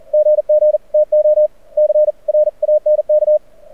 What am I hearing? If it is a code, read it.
Morse code, heard as a keyed tone, about 42 wpm, 590 Hz. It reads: GG TO C RANK
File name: archived / CRSnet